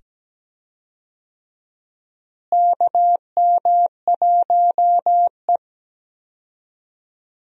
Morse code, heard as a keyed tone, about 17 wpm, 700 Hz. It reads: KM1E
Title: USA Call Signs - 4 Characters - Repeated 2x - 17wpm